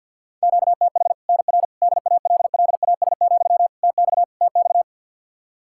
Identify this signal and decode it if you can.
Morse code, heard as a keyed tone, about 52 wpm, 700 Hz. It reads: QTH NR DALLAS, TX TX
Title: Practice QSO - Exchange 1 - Beginning - 50wpm